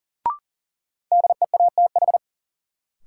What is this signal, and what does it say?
Morse code, heard as a keyed tone, about 40 wpm, 700 Hz. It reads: DEATH
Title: Sentences from Top 1100 Words - New Word Drill Alt - 40wpm